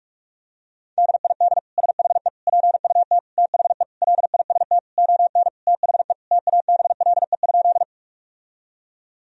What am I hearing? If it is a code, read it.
Morse code, heard as a keyed tone, about 41 wpm, 700 Hz. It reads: DID SHE PUT THE LIST ON THE TABLE?
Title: Sentences from Top 1000 Words - Review Part 2 - 45wpm